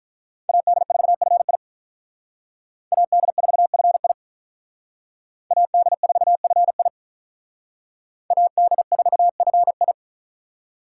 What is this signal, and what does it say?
Morse code, heard as a keyed tone, about 43 wpm, 700 Hz. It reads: AD4FI AD4FI AD4FI AD4FI
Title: Speed Racing - USA Call Signs 35wpm